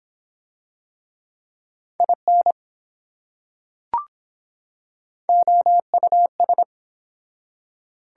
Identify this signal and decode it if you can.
Morse code, heard as a keyed tone, about 24 wpm, 700 Hz. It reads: IN OUS